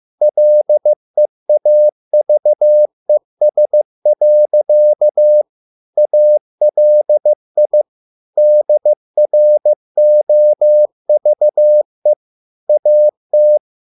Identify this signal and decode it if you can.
Morse code, heard as a keyed tone, about 15 wpm, 600 Hz. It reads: LEAVES. ALI DROVE AT